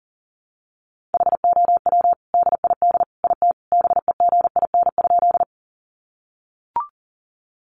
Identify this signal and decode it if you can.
Morse code, heard as a keyed tone, about 40 wpm, 700 Hz. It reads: HOW DID IT BEGIN?
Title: Sentences from Top 300 Words 40wpm